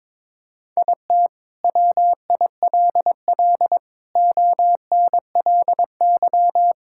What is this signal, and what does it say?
Morse code, heard as a keyed tone, about 22 wpm, 700 Hz. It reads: IT WILL ONLY
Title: Sentences from Top 1000 Words - Review Part 5 - 22wpm